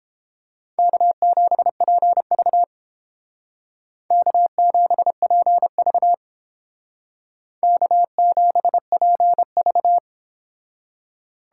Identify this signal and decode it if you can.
Morse code, heard as a keyed tone, about 30 wpm, 700 Hz. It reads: K7PV K7PV K7PV